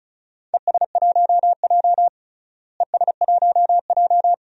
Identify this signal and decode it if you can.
Morse code, heard as a keyed tone, about 36 wpm, 700 Hz. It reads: ES1J ES1J
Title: DXCC-100 Call Signs - 4 Characters - Repeated 2x - 35wpm